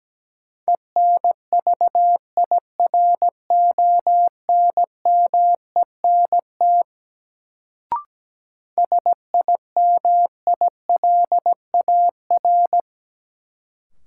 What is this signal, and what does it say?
Morse code, heard as a keyed tone, about 17 wpm, 700 Hz. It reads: ENVIRONMENT SIMILAR
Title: Sentences from Top 1000 Words - Review Part 5 - New Word Drill - 17wpm